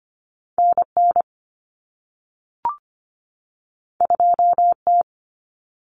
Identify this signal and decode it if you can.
Morse code, heard as a keyed tone, about 25 wpm, 700 Hz. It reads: NN 2T
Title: Common Cut Numbers - 2-Digit 25wpm